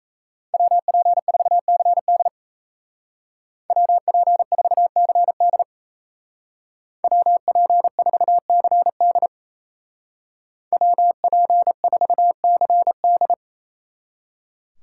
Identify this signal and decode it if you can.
Morse code, heard as a keyed tone, about 34 wpm, 700 Hz. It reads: WP4CD WP4CD WP4CD WP4CD
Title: Speed Racing - USA 2x2 Call Signs 28wpm